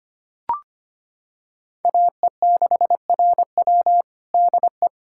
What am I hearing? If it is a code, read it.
Morse code, heard as a keyed tone, about 25 wpm, 700 Hz. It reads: AE6RW DE
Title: USA Callsign DE Callsign - 5 Characters - 25wpm